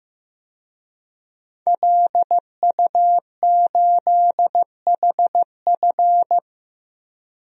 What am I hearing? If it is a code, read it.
Morse code, heard as a keyed tone, about 16 wpm, 700 Hz. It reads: LU8HF
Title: DXCC-100 Call Signs - 5 Characters - 15wpm